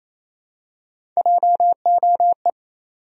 Morse code, heard as a keyed tone, about 28 wpm, 700 Hz. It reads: JOE
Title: USA Common Names - Max 3 Letters - 28wpm